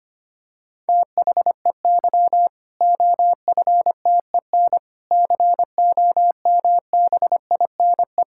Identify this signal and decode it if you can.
Morse code, heard as a keyed tone, about 25 wpm, 700 Hz. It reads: THEY OFTEN COMBINE
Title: Sentences from Top 1600 Words 25wpm